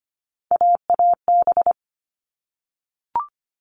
Morse code, heard as a keyed tone, about 25 wpm, 700 Hz. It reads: AAB
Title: All Cut Numbers - 3-Digit 25wpm